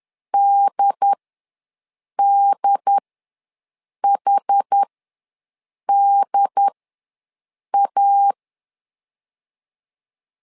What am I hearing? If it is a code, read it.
Morse code, heard as a keyed tone, about 10 wpm, spaced as for 6 wpm, 800 Hz. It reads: DDHDA
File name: LEZIONI / 2 UHDN / 2 UHD ws 10wpm 2058